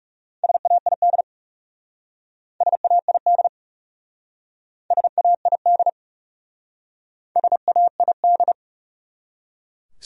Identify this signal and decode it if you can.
Morse code, heard as a keyed tone, about 38 wpm, 700 Hz. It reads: SAID SAID SAID SAID
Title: Speed-Racing Top 100 Words 30wpm